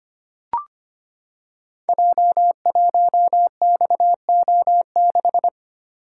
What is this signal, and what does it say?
Morse code, heard as a keyed tone, about 25 wpm, 700 Hz. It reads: J1XO6
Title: Code Groups - 5 Letter:Numbers - 25wpm